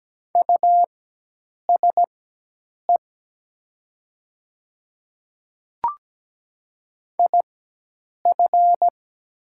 Morse code, heard as a keyed tone, about 17 wpm, spaced as for 8 wpm, 700 Hz. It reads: USE IF
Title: 4x Character Spacing - Top 100 Words - 17wpm